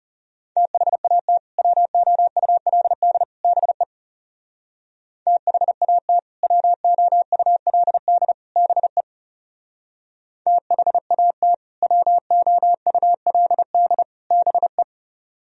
Morse code, heard as a keyed tone, about 37 wpm, 700 Hz. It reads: THAT WOULD BE THAT WOULD BE THAT WOULD BE
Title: Speed-Racing - Sets of 3 Words 30wpm